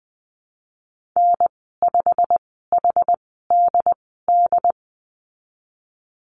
Morse code, heard as a keyed tone, about 20 wpm, spaced as for 15 wpm, 700 Hz. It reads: N5HDD
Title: 186 Lesson 30g - Letter b - Call Signs 20wpm